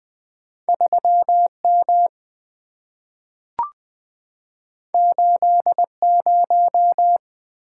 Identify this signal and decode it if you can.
Morse code, heard as a keyed tone, about 20 wpm, 700 Hz. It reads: 3 M 8 0
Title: Two Letter-Number Combinations - 20wpm